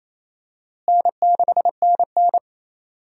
Morse code, heard as a keyed tone, about 28 wpm, 700 Hz. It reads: N6NN
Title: USA Call Signs - 4 Characters - Repeated 2x - 28wpm